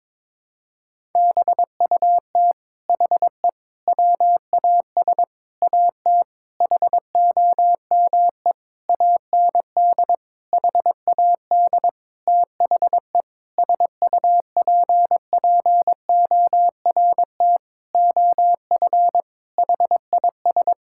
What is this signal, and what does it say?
Morse code, heard as a keyed tone, about 22 wpm, 700 Hz. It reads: BUT HE WAS AT HOME AND HAD THE SUPPORT OF HIS
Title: Sentences from Top 1000 Words - Review Part 5 - 22wpm